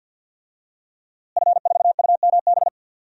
Morse code, heard as a keyed tone, about 51 wpm, 700 Hz. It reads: F4UMB
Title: DXCC-100 Call Signs - 5 Characters - Repeated 3x - 50wpm